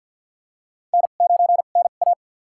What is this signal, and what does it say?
Morse code, heard as a keyed tone, about 52 wpm, spaced as for 37 wpm, 700 Hz. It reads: N9NA
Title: USA Call Signs - 4 Characters - 2x Character Spacing - 50wpm